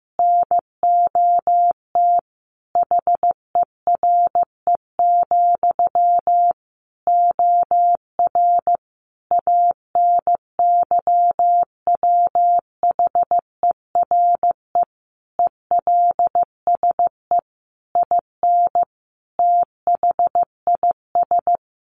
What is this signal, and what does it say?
Morse code, heard as a keyed tone, about 15 wpm, 700 Hz. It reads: NOT HERE, OR ANYWHERE ELSE IN THIS
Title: Sentences from Top 1400 Words 15wpm